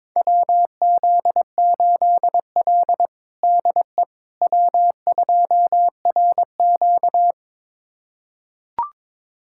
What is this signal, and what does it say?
Morse code, heard as a keyed tone, about 22 wpm, 700 Hz. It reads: WZ8L DE W2RQ
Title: USA Callsign DE Callsign - 4 Characters - 22wpm